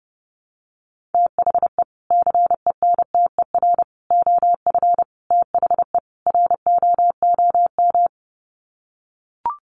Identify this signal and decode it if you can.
Morse code, heard as a keyed tone, about 30 wpm, 700 Hz. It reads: THE CENTER OF THE ROOM
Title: Sets of 5 Words Encore 30wpm